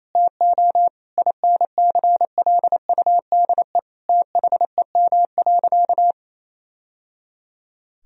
Morse code, heard as a keyed tone, about 28 wpm, 700 Hz. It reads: TO INCLUDE THEM.
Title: Sentences from Top 1000 Words - Review Part 3 - 28wpm